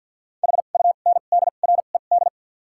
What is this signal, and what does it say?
Morse code, heard as a keyed tone, about 53 wpm, spaced as for 37 wpm, 700 Hz. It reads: HUNDRED
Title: 2x Character Spacing - Top 500 Words - 50wpm